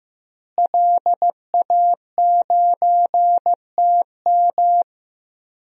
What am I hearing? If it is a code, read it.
Morse code, heard as a keyed tone, about 15 wpm, 700 Hz. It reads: LA9TM
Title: DXCC-100 Call Signs - 5 Characters - Repeated 2x - 15wpm